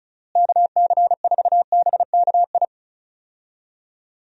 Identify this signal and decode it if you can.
Morse code, heard as a keyed tone, about 36 wpm, 700 Hz. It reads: KC4BKI
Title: USA Call Signs - 6 Characters - Repeated 3x - 35wpm